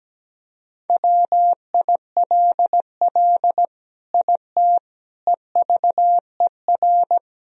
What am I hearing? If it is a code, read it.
Morse code, heard as a keyed tone, about 17 wpm, 700 Hz. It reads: WILL IT EVER
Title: Sentences from Top 1000 Words - Review Part 2 - 17wpm